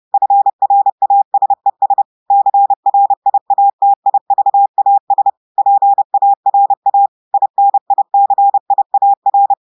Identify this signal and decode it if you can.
Morse code, heard as a keyed tone, about 30 wpm, 800 Hz. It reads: FRASES CRIATIVAS PARA INICIAR
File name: audios / portugais / portugais texte en clair 2 vitesse 30 mots minutes